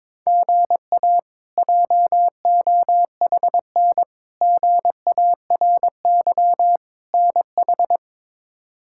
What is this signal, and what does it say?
Morse code, heard as a keyed tone, about 22 wpm, 700 Hz. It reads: GA JOHN GARY NH
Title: K1USN SST QSO Practice - 22wpm